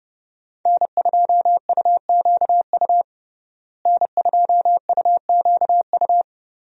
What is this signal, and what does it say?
Morse code, heard as a keyed tone, about 31 wpm, 700 Hz. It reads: N2UQU N2UQU
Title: USA Call Signs - 5 Characters - Repeated 2x - 30wpm